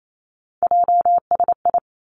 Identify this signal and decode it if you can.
Morse code, heard as a keyed tone, about 28 wpm, 700 Hz. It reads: JSI
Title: Three Challenging Letter-Number Combinations 28wpm